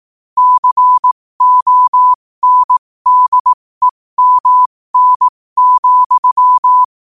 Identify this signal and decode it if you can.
Morse code, heard as a keyed tone, about 18 wpm, spaced as for 15 wpm, 1000 Hz. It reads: CONDEMN,
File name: qotdfiles / 15wpm